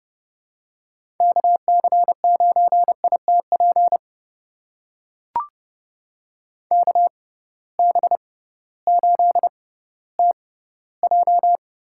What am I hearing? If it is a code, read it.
Morse code, heard as a keyed tone, about 30 wpm, spaced as for 16 wpm, 700 Hz. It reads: KC9ITP KB8TJ
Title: USA Call Signs - 6 Characters - 6x Character Spacing - 30wpm